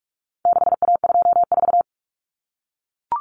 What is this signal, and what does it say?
Morse code, heard as a keyed tone, about 44 wpm, 700 Hz. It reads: 6A24
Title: Common Cut Numbers - 4-Digit 45wpm